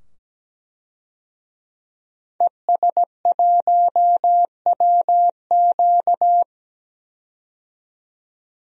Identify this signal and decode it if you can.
Morse code, heard as a keyed tone, about 17 wpm, 700 Hz. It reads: ES1WQ
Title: DXCC-100 Call Signs - 5 Characters - 17wpm